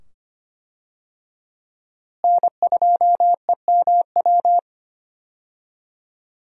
Morse code, heard as a keyed tone, about 25 wpm, 700 Hz. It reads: N2EMW
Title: USA Call Signs - 5 Characters - Repeated 2x - 25wpm